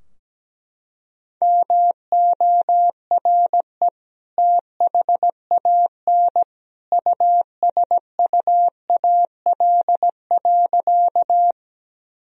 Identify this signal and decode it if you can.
Morse code, heard as a keyed tone, about 17 wpm, 700 Hz. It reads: MORE THAN USUAL.
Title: Sentences from Top 1000 Words - Review Part 2 - 17wpm